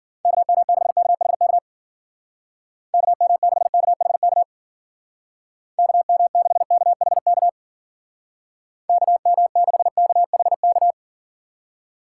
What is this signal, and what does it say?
Morse code, heard as a keyed tone, about 49 wpm, 700 Hz. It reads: KM6KHK KM6KHK KM6KHK KM6KHK
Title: Speed Racing - USA 2x3 Call Signs 40wpm